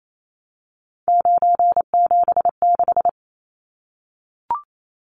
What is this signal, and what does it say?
Morse code, heard as a keyed tone, about 28 wpm, 700 Hz. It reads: 976
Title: Numbers - 3-Digit 28wpm